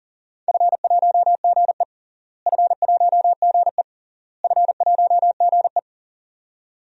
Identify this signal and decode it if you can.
Morse code, heard as a keyed tone, about 40 wpm, 700 Hz. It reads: F1GE F1GE F1GE
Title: DXCC-100 Call Signs - 4 Characters - Repeated 3x - 40wpm